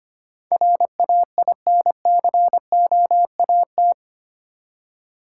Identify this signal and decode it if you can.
Morse code, heard as a keyed tone, about 25 wpm, 700 Hz. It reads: RAINCOAT
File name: practice / 25wpm / Easy Words with Hints - 25wpm